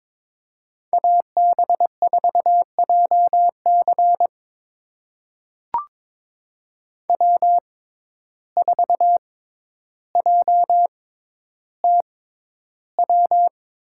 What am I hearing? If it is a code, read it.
Morse code, heard as a keyed tone, about 22 wpm, spaced as for 11 wpm, 700 Hz. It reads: AB4JC W4JTW
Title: USA Call Signs - 5 Characters - 6x Character Spacing - 22wpm